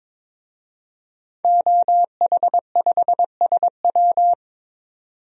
Morse code, heard as a keyed tone, about 22 wpm, 700 Hz. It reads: OH5SW
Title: DXCC-100 Call Signs - 5 Characters - 22wpm